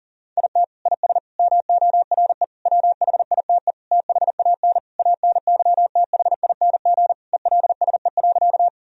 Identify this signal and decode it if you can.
Morse code, heard as a keyed tone, about 40 wpm, 700 Hz. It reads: IT IS MORE WHITE THAN ANYTHING ELSE.
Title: Sentences from Top 1000 Words - Review Part 1 - 40wpm